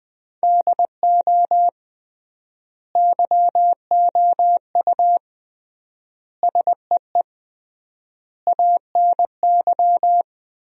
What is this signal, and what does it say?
Morse code, heard as a keyed tone, about 20 wpm, 700 Hz. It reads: DO YOU SEE ANY
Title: 3x Word-Spacing - Sentences from Top 500 Words - 20wpm